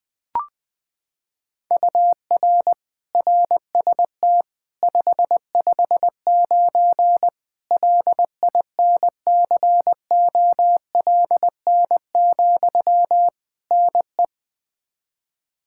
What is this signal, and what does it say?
Morse code, heard as a keyed tone, about 20 wpm, 700 Hz. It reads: UR RST 559 LINCOLN, NE
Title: Practice QSO - Exchange 1 - Beginning - 20wpm